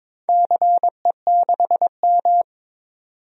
Morse code, heard as a keyed tone, about 22 wpm, 700 Hz. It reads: CE6M
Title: DXCC-100 Call Signs - 4 Characters - 22wpm